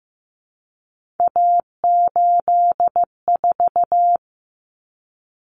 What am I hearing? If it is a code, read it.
Morse code, heard as a keyed tone, about 15 wpm, 700 Hz. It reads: A84
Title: Common Cut Numbers - 3-Digit 15wpm